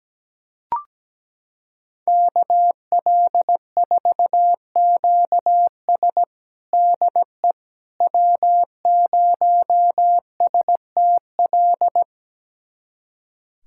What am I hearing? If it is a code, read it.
Morse code, heard as a keyed tone, about 17 wpm, 700 Hz. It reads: KL4QS DE W0STL
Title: USA Callsign DE Callsign - 5 Characters - 17wpm